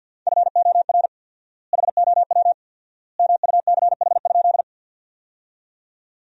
Morse code, heard as a keyed tone, about 50 wpm, 700 Hz. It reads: FOR HOW MUCH?
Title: 4x Word-Spacing - Sentences from Top 200 Words - 50wpm